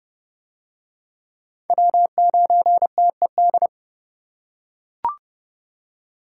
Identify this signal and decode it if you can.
Morse code, heard as a keyed tone, about 30 wpm, 700 Hz. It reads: W9TED